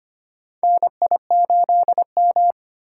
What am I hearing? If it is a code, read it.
Morse code, heard as a keyed tone, about 25 wpm, 700 Hz. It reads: NI8M
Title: USA Call Signs - 4 Characters - 8x Character Spacing - 25wpm